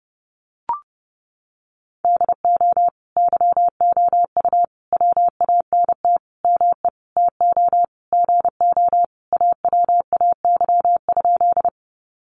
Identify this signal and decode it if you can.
Morse code, heard as a keyed tone, about 30 wpm, 700 Hz. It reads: DO YOU WANT ME TO GO AWAY?
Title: Sentences from Top 600 Words 30wpm